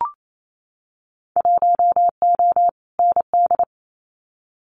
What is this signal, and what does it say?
Morse code, heard as a keyed tone, about 28 wpm, 700 Hz. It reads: 1O ND
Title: Winter Field Day Exchanges 28wpm